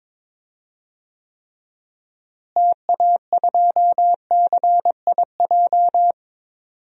Morse code, heard as a keyed tone, about 21 wpm, 700 Hz. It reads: TA2CIJ
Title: DXCC-100 Call Signs - 6 Characters - Repeated 3x - 22wpm